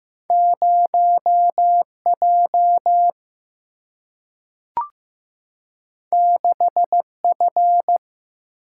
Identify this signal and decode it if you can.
Morse code, heard as a keyed tone, about 15 wpm, 700 Hz. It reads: 0J 6F
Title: Single-Letter Single-Number 15wpm